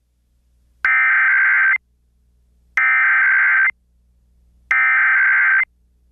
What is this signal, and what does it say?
Emergency Alert System (SAME) tones (header bursts), heard as harsh buzzing bursts. It is heard from 0.9 to 5.6 s.